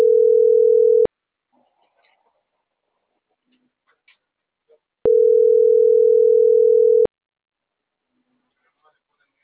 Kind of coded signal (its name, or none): telephone call progress tone